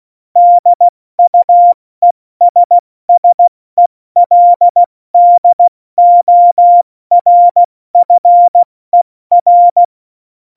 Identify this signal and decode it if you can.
Morse code, heard as a keyed tone, about 16 wpm, 700 Hz. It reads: DUESSELDORFER
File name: Morsekurs / Witze / Witz 08. Oktober 2019, 70 F80